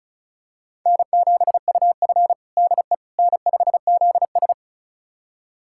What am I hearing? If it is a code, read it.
Morse code, heard as a keyed tone, about 36 wpm, 700 Hz. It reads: N7UF DE N5ZS
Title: USA Callsign DE Callsign - 4 Characters - 35wpm